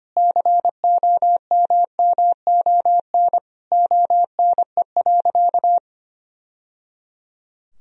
Morse code, heard as a keyed tone, about 25 wpm, 700 Hz. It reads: COMMON ONE.
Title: Sentences from Top 1000 Words - Review Part 4 - 25wpm